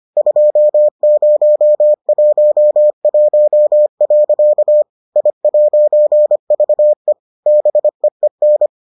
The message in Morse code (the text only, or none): 2011. I'VE BEEN